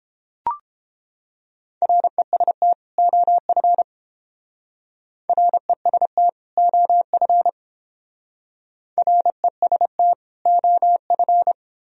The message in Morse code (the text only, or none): REST OF REST OF REST OF